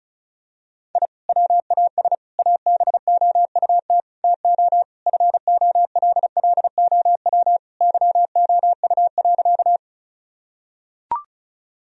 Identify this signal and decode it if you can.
Morse code, heard as a keyed tone, about 36 wpm, 700 Hz. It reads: I WAS ABOUT TO FOLLOW YOU.
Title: Sentences from Top 1000 Words - Review Part 1 - 35wpm